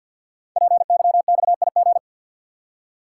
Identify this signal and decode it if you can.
Morse code, heard as a keyed tone, about 51 wpm, 700 Hz. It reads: PYXIG
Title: Code Groups - 5 Letters - 50wpm